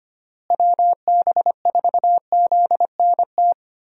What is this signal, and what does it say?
Morse code, heard as a keyed tone, about 25 wpm, 700 Hz. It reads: WB4ZNT
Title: USA Call Signs - 6 Characters - Repeated 2x - 25wpm